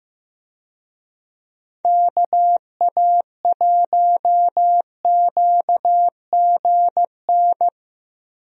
Morse code, heard as a keyed tone, about 15 wpm, 700 Hz. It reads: KA1QGN
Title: USA Call Signs - 6 Characters - Repeated 3x - 15wpm